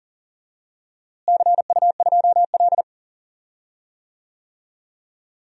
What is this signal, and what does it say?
Morse code, heard as a keyed tone, about 40 wpm, 700 Hz. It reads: CU2L
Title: DXCC-100 Call Signs - 4 Characters - Repeated 3x - 40wpm